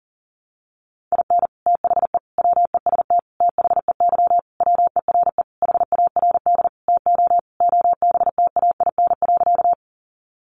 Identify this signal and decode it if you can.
Morse code, heard as a keyed tone, about 40 wpm, 700 Hz. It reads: IN THE WEST THEY WERE HARD TO OBTAIN.
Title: Sentences from Top 1300 Words 40wpm